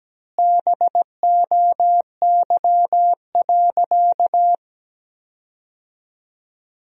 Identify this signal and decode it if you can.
Morse code, heard as a keyed tone, about 17 wpm, 700 Hz. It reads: BOY.
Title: 3x Word-Spacing - Sentences from Top 200 Words - 17wpm